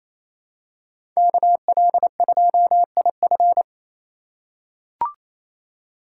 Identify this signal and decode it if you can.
Morse code, heard as a keyed tone, about 28 wpm, 700 Hz. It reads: KL2IF
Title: USA Call Signs - 5 Characters - 4x Character Spacing - 28wpm